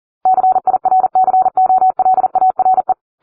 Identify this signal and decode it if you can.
Morse code, heard as a keyed tone, about 40 wpm, 750 Hz. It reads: CIRCOLARE